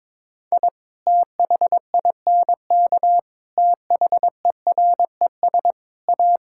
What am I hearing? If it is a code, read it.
Morse code, heard as a keyed tone, about 22 wpm, 700 Hz. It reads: I THINK THERES A
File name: practice / 22wpm / Sets of 6 Words - 22wpm